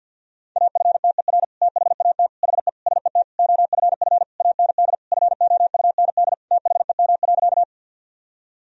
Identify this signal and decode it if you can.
Morse code, heard as a keyed tone, about 52 wpm, 700 Hz. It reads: AFTER THAT HE SET OFF AND FOUND THEM.
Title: Sentences from Top 1000 Words - Review Part 1 - 50wpm